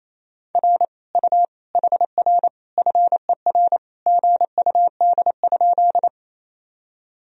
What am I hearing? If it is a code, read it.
Morse code, heard as a keyed tone, about 28 wpm, 700 Hz. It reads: R U HR FER GUD?
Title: Sentences from Top 200 Words and Abbreviations 28wpm